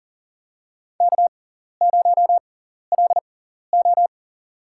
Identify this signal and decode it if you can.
Morse code, heard as a keyed tone, about 40 wpm, spaced as for 14 wpm, 700 Hz. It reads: K0LO
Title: USA Call Signs - 6x Character Spacing - 40wpm